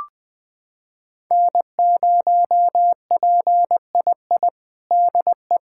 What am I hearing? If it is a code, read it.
Morse code, heard as a keyed tone, about 20 wpm, 700 Hz. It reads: N0PII DE
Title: USA Callsign DE Callsign - 5 Characters - 20wpm